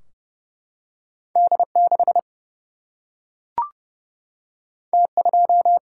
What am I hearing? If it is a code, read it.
Morse code, heard as a keyed tone, about 30 wpm, 700 Hz. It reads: D6 T2
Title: Single-Letter Single-Number 30wpm